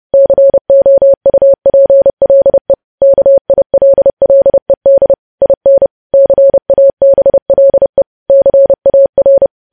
Morse code, heard as a keyed tone, about 30 wpm, 550 Hz. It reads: COUPLE KILLED IN CABLE CAR
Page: News Headlines in Morse Code at 30 WPM